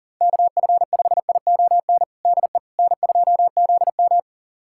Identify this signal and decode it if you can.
Morse code, heard as a keyed tone, about 40 wpm, 700 Hz. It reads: KF5ION DE N2ZM
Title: USA Callsign DE Callsign 40wpm